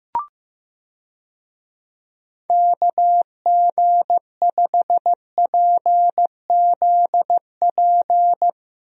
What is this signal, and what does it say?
Morse code, heard as a keyed tone, about 15 wpm, 700 Hz. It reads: KG5PZP
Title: USA Call Signs - 6 Characters - Repeated 3x - 15wpm